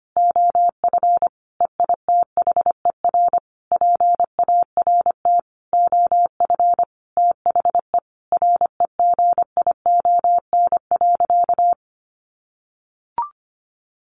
Morse code, heard as a keyed tone, about 25 wpm, 700 Hz. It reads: OF EITHER PART OF THE REGION.
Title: Sentences from Top 1500 Words 25wpm